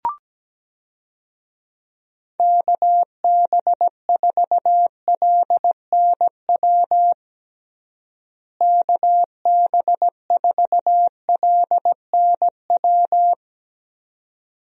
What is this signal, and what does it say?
Morse code, heard as a keyed tone, about 17 wpm, 700 Hz. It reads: KB4LNW KB4LNW
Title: USA Call Signs - 6 Characters - Repeated 2x - 17wpm